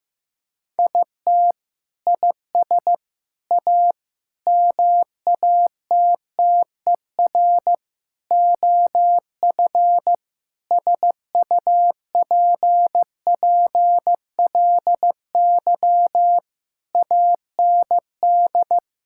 Morse code, heard as a keyed tone, about 15 wpm, 700 Hz. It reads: IT IS A MATTER OF SUPPLY AND
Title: Sentences from Top 1000 Words - Review Part 5 - 15wpm